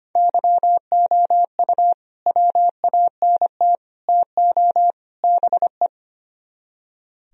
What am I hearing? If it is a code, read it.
Morse code, heard as a keyed tone, about 33 wpm, 700 Hz. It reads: YOU WANT TO BE